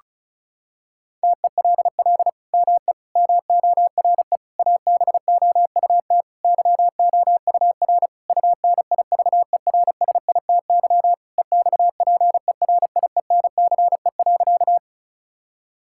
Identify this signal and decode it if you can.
Morse code, heard as a keyed tone, about 36 wpm, 700 Hz. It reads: TELL ME MORE ABOUT YOUR UNIVERSITY EXPERIENCE.
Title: Sentences from Top 1000 Words - Review Part 5 - 35wpm